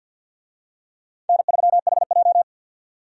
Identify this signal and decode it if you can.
Morse code, heard as a keyed tone, about 51 wpm, 700 Hz. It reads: N3HJ
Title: USA Call Signs - 4 Characters - 50wpm